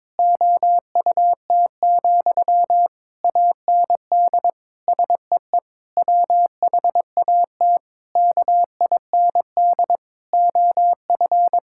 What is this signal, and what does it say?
Morse code, heard as a keyed tone, about 22 wpm, 700 Hz. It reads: OUT, AND SEE WHAT KIND OF
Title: Sentences from Top 1000 Words - Review Part 6 - 22wpm